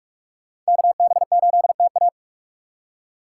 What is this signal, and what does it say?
Morse code, heard as a keyed tone, about 44 wpm, 700 Hz. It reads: KB8TA